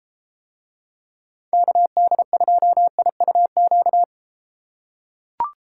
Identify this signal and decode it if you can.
Morse code, heard as a keyed tone, about 32 wpm, 700 Hz. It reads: KD2IUQ